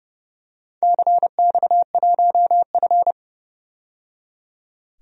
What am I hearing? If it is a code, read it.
Morse code, heard as a keyed tone, about 28 wpm, 700 Hz. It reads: CX1F
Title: DXCC-100 Call Signs - 4 Characters - 30wpm